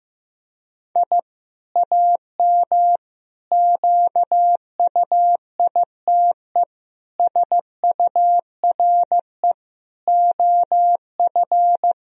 Morse code, heard as a keyed tone, about 16 wpm, 700 Hz. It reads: I AM QUITE SURE OF
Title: Sentences from Top 1000 Words - Review Part 2 - 15wpm